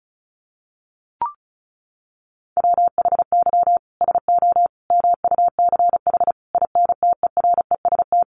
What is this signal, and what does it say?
Morse code, heard as a keyed tone, about 34 wpm, 700 Hz. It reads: WHY SO MUCH INTEREST
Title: Sentences from Top 1200 Words 35wpm